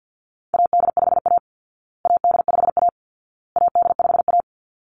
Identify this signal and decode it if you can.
Morse code, heard as a keyed tone, about 50 wpm, 700 Hz. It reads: AD5A AD5A AD5A
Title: Call Signs Repeated 2x - 50wpm